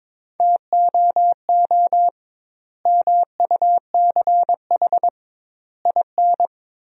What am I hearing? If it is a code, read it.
Morse code, heard as a keyed tone, about 22 wpm, 700 Hz. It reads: TOO MUCH IN
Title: 2x Word-Spacing - Sentences from Top 200 Words - 22wpm